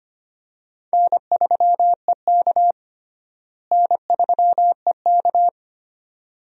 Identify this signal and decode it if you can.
Morse code, heard as a keyed tone, about 25 wpm, 700 Hz. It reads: N3EK N3EK
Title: USA Call Signs - 4 Characters - Repeated 2x - 25wpm